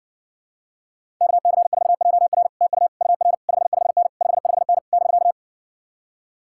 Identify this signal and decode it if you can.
Morse code, heard as a keyed tone, about 66 wpm, 700 Hz. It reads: KC4JR TU UR 55N 55N <BK>